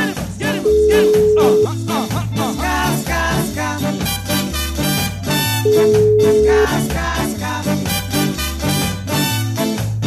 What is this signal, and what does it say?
Ringback tone (Europe, 425 Hz), with noise.